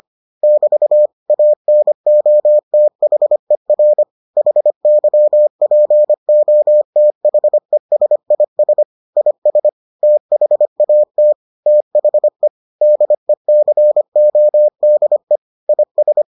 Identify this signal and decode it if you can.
Morse code, heard as a keyed tone, about 25 wpm, 600 Hz. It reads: <BT> ANOTHER HYPOTHESIS IS THAT THE DECODE IS